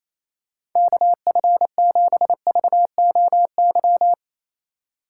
Morse code, heard as a keyed tone, about 28 wpm, 700 Hz. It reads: KF7VOY
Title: USA Call Signs - 6 Characters - 4x Character Spacing - 28wpm